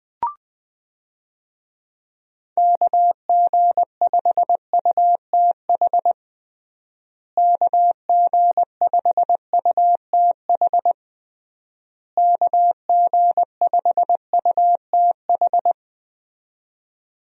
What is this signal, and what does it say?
Morse code, heard as a keyed tone, about 20 wpm, 700 Hz. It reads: KG5UTH KG5UTH KG5UTH
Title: USA Call Signs - 6 Characters - Repeated 3x - 20wpm